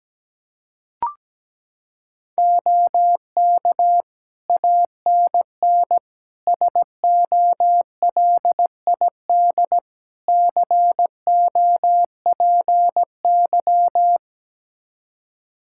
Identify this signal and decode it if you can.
Morse code, heard as a keyed tone, about 17 wpm, 700 Hz. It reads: OK ANN SOLID COPY
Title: K1USN SST QSO Practice 17wpm